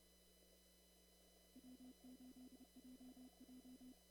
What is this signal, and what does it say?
Morse code, heard as a keyed tone, about 30 wpm, 250 Hz. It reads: W8JJ
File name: CW / kt5j wpx cw-op-k5pi